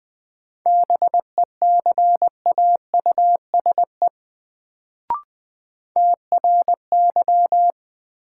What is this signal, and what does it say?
Morse code, heard as a keyed tone, about 20 wpm, 700 Hz. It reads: BECAUSE TRY
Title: Sets of 2 Words - New Word Drill - 20wpm